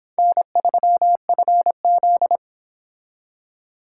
Morse code, heard as a keyed tone, about 26 wpm, 700 Hz. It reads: N3FZ